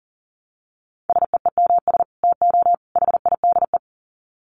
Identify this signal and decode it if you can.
Morse code, heard as a keyed tone, about 40 wpm, 700 Hz. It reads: SEEMS TO HIDE
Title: Dit Challenge 40wpm